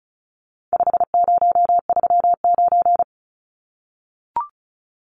Morse code, heard as a keyed tone, about 34 wpm, 700 Hz. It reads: 5039
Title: Numbers - 4-Digit 35wpm